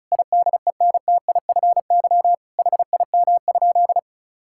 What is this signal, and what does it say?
Morse code, heard as a keyed tone, about 36 wpm, 700 Hz. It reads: IDENTIFY HIM?
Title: Sentences from Top 1000 Words - Review Part 6 - 35wpm